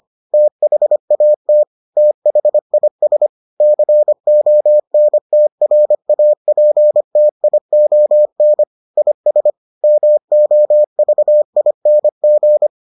Morse code, heard as a keyed tone, about 25 wpm, 600 Hz. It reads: THAT THIS CONTRAPTION IS MOVING